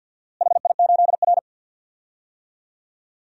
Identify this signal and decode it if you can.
Morse code, heard as a keyed tone, about 49 wpm, 700 Hz. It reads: HI8R
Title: DXCC-100 Call Signs - 4 Characters - Repeated 2x - 50wpm